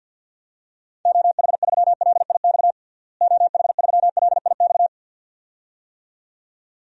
Morse code, heard as a keyed tone, about 52 wpm, 700 Hz. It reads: OH3LIX OH3LIX
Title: DXCC-100 Call Signs - 6 Characters - Repeated 2x - 50wpm